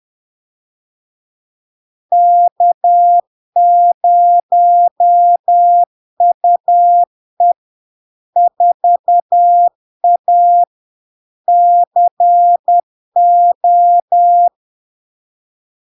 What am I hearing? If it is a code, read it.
Morse code, heard as a keyed tone, about 10 wpm, 700 Hz. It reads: K0UE 4A CO
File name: LARK-FD-2022-Log-in-CW-10wpm-700-Hz